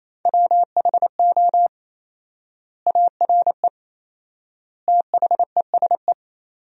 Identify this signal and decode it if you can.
Morse code, heard as a keyed tone, about 28 wpm, 700 Hz. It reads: WHO ARE THESE
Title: 4x Word-Spacing - Sentences from Top 200 Words - 28wpm